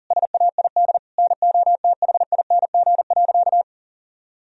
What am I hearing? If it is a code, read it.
Morse code, heard as a keyed tone, about 40 wpm, 700 Hz. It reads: SAID NOTHING.
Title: Sentences from Top 1000 Words - Review Part 2 - 40wpm